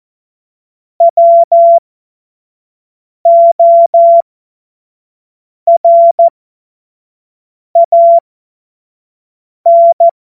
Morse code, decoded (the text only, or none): WORAN